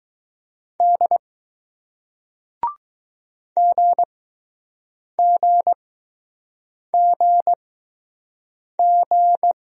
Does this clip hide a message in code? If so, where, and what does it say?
Yes: Morse code from 0.8 to 9.5 s, heard as a keyed tone, about 20 wpm, spaced as for 6 wpm, 700 Hz. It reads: D GGGG